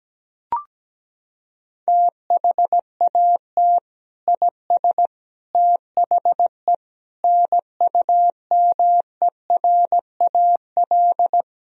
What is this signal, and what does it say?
Morse code, heard as a keyed tone, about 17 wpm, 700 Hz. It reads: THAT IS THE NUMERAL
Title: Sentences from Top 1000 Words - Review Part 2 - 17wpm